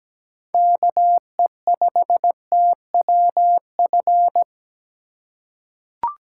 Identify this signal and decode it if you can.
Morse code, heard as a keyed tone, about 17 wpm, 700 Hz. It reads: KE5TWF
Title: USA Call Signs - 6 Characters - 2x Character Spacing - 17wpm